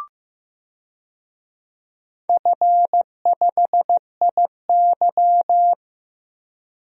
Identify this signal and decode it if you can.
Morse code, heard as a keyed tone, about 15 wpm, 700 Hz. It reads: F5IY
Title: DXCC-100 Call Signs - 4 Characters - 15wpm